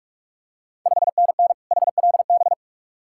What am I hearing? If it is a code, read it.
Morse code, heard as a keyed tone, about 46 wpm, 690 Hz. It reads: 5NN HLB
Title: Ontario QSO Party Exchange 45wpm